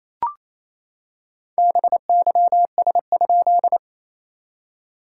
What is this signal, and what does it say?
Morse code, heard as a keyed tone, about 27 wpm, 690 Hz. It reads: BYS?
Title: SOTA QSO Practice 28wpm